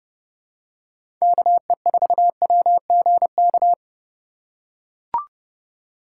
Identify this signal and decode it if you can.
Morse code, heard as a keyed tone, about 30 wpm, 700 Hz. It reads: KE4WGK
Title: USA Call Signs - 6 Characters - 2x Character Spacing - 30wpm